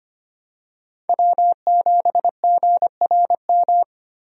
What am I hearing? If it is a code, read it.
Morse code, heard as a keyed tone, about 25 wpm, 700 Hz. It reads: W7GRM
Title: USA Call Signs - 5 Characters - Repeated 2x - 25wpm